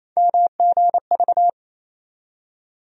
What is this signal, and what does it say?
Morse code, heard as a keyed tone, about 28 wpm, 700 Hz. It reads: MGV
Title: USA Call Sign Postfixes - 3 Letters 28wpm